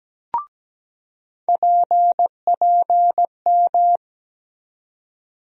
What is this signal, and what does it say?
Morse code, heard as a keyed tone, about 17 wpm, 700 Hz. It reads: PPM
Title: USA Call Sign Postfixes - 3 Letters 17wpm